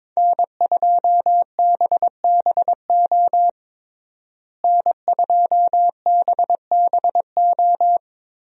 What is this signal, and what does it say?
Morse code, heard as a keyed tone, about 22 wpm, 700 Hz. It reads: N2BBO N2BBO
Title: USA Call Signs - 5 Characters - Repeated 3x - 22wpm